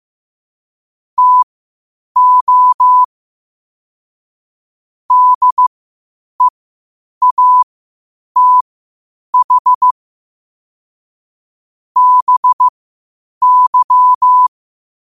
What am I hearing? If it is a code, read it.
Morse code, heard as a keyed tone, about 15 wpm, spaced as for 8 wpm, 1000 Hz. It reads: TO DEATH BY
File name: qotdfiles / 8wpm